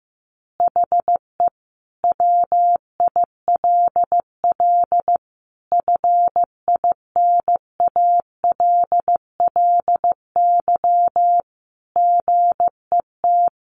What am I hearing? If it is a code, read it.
Morse code, heard as a keyed tone, about 15 wpm, 700 Hz. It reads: HE WILL FINALLY GET
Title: Additional Easy Phrases 15wpm